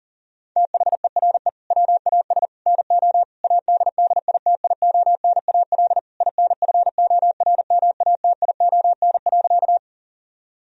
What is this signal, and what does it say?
Morse code, heard as a keyed tone, about 37 wpm, 700 Hz. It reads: THERE WAS NO ADDITIONAL INFORMATION.
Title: Sentences from Top 1000 Words - Review Part 4 - 40wpm